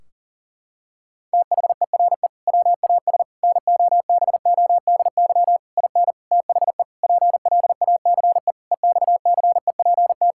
Morse code, heard as a keyed tone, about 40 wpm, 700 Hz. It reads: THERE WAS NOBODY IN THE PLACE EXCEPT
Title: Sentences from Top 1600 Words 40wpm